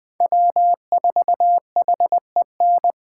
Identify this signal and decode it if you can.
Morse code, heard as a keyed tone, about 21 wpm, 700 Hz. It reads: W4HEN